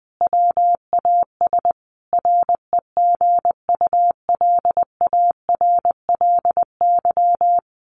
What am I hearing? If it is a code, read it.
Morse code, heard as a keyed tone, about 20 wpm, 700 Hz. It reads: WAS REGULARLY
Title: Sentences from Top 1400 Words 20wpm